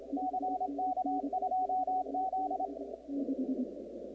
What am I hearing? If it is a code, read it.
Morse code, heard as a keyed tone, about 27 wpm, 740 Hz, with radio noise. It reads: CK2Z
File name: contests / 2019-cw-wpx-kt5j-k5pi- -2330 0-38